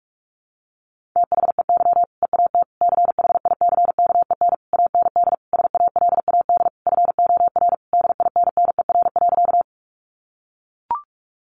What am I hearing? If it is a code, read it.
Morse code, heard as a keyed tone, about 44 wpm, 700 Hz. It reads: THEY EAT CHICKEN AND SALAD FOR DINNER.
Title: Easy Phrases 45wpm